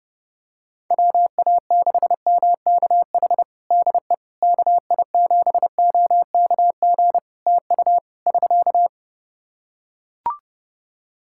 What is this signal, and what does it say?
Morse code, heard as a keyed tone, about 32 wpm, 700 Hz. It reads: WA6MKH DE KI7OKG TU <SK>
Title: Practice QSO - Exchange 4 - Ending - 30wpm